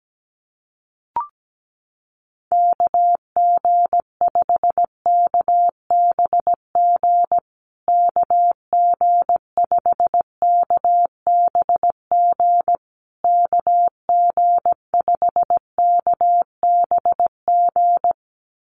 Morse code, heard as a keyed tone, about 17 wpm, 700 Hz. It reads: KG5KBG KG5KBG KG5KBG
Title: USA Call Signs Repeated 3x 17wpm